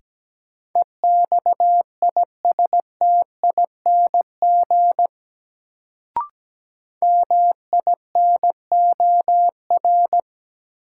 Morse code, heard as a keyed tone, about 17 wpm, 700 Hz. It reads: EXISTING MINOR
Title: Sentences from Top 1500 Words - New Word Drill - 17wpm